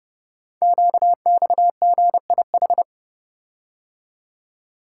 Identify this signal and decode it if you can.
Morse code, heard as a keyed tone, about 30 wpm, 700 Hz. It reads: QXGIH
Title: Code Groups - 5 Letters - 30wpm